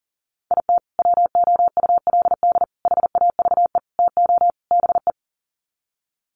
Morse code, heard as a keyed tone, about 37 wpm, 700 Hz. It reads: IT WOULD HAVE TO BE
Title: Sets of 5 Words Encore 40wpm